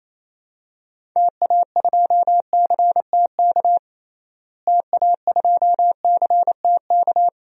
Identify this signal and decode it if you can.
Morse code, heard as a keyed tone, about 28 wpm, 700 Hz. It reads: TA2CTK TA2CTK
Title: DXCC-100 Call Signs - 6 Characters - Repeated 3x - 28wpm